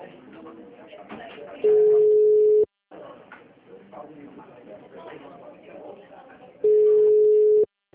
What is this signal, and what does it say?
Ringback tone (Europe, 425 Hz), with speech.